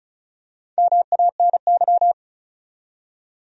Morse code, heard as a keyed tone, about 36 wpm, 700 Hz. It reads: MANY
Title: Top 100 Words 35wpm